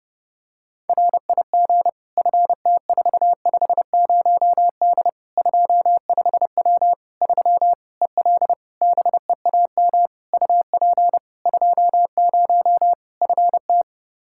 Morse code, heard as a keyed tone, about 30 wpm, 700 Hz. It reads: RIG FT450D 25W 3 EL BEAM UP 20 FT
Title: Practice QSO - Exchange 2 - Info - 30wpm